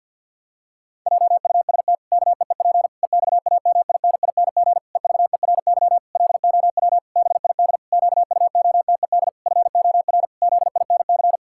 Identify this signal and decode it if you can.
Morse code, heard as a keyed tone, about 50 wpm, 700 Hz. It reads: JUST KEEP EXAMINING EVERY LOW BID QUOTED FOR ZINC
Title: Pangram Challenge v2 - 50wpm